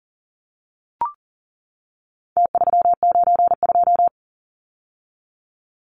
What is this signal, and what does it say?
Morse code, heard as a keyed tone, about 40 wpm, 700 Hz. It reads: T392
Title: Numbers - Contest Style - 4-Digit 40wpm